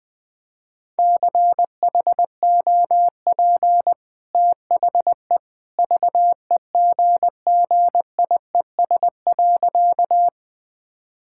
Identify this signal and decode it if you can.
Morse code, heard as a keyed tone, about 21 wpm, 700 Hz. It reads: CHOP THE VEGGIES.
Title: Easy Sentences with Hints - 20wpm